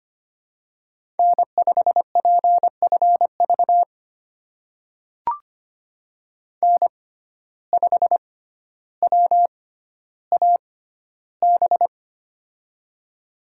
Morse code, heard as a keyed tone, about 25 wpm, spaced as for 13 wpm, 700 Hz. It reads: N5PFV N5WAB
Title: USA Call Signs - 5 Characters - 6x Character Spacing - 25wpm